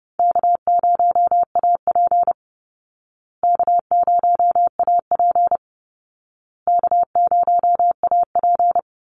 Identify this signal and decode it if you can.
Morse code, heard as a keyed tone, about 30 wpm, 700 Hz. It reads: K0AP K0AP K0AP
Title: Call Signs Repeated 2x - 30wpm